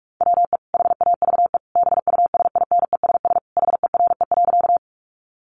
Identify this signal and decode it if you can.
Morse code, heard as a keyed tone, about 44 wpm, 700 Hz. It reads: WE HAVE BUSINESS HERE.
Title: Sentences from Top 600 Words 45wpm